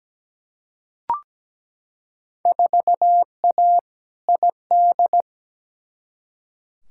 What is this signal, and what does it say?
Morse code, heard as a keyed tone, about 17 wpm, 700 Hz. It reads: 4A ID
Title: Field Day Exchanges - 17wpm